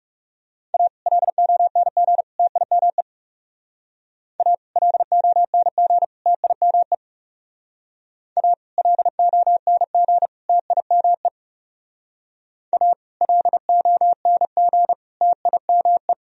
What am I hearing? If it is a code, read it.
Morse code, heard as a keyed tone, about 35 wpm, 700 Hz. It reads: A LONG TIME A LONG TIME A LONG TIME A LONG TIME
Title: Speed-Racing - Sets of 3 Words 30wpm